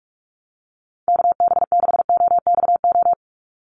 Morse code, heard as a keyed tone, about 44 wpm, 700 Hz. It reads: KB6OXO
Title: USA Call Signs 45wpm